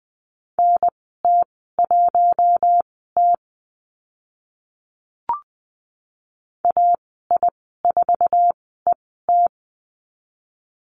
Morse code, heard as a keyed tone, about 20 wpm, spaced as for 15 wpm, 700 Hz. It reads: NT1T AI4ET
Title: 048 Lesson 9g - Number 4 - Call Signs 20wpm